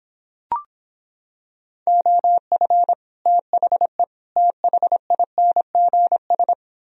Morse code, heard as a keyed tone, about 26 wpm, 700 Hz. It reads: OF THE THINGS